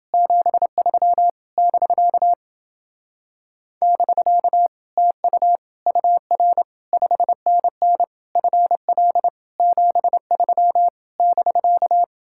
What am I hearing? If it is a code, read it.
Morse code, heard as a keyed tone, about 29 wpm, 700 Hz. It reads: 73 <BK> <BK> TU UR 5NN FL 73 <BK>